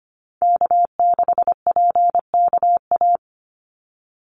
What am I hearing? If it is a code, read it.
Morse code, heard as a keyed tone, about 25 wpm, 700 Hz. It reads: K6PKA
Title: USA Call Signs 25wpm